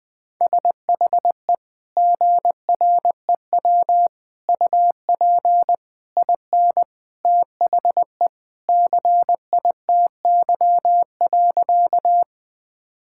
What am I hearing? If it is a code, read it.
Morse code, heard as a keyed tone, about 20 wpm, 700 Hz. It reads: SHE GREW UP IN THE CITY.
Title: Sentences with 2-Word Phrases - 20wpm